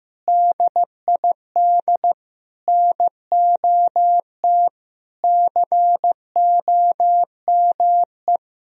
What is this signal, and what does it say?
Morse code, heard as a keyed tone, about 16 wpm, 700 Hz. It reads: DID NOT COME
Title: Sentences from Top 1000 Words - Review Part 3 - 15wpm